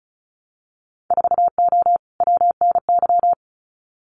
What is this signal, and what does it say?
Morse code, heard as a keyed tone, about 36 wpm, 700 Hz. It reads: 4O WNY
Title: Winter Field Day Exchanges Encore 35wpm